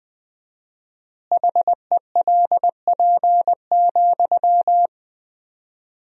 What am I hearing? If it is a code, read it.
Morse code, heard as a keyed tone, about 20 wpm, 700 Hz. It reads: HELP,
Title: 4x Word-Spacing - Sentences from Top 300 Words - 20wpm